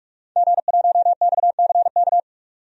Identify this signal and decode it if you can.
Morse code, heard as a keyed tone, about 46 wpm, 700 Hz. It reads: G1XCK
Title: DXCC-100 Call Signs - 5 Characters - 45wpm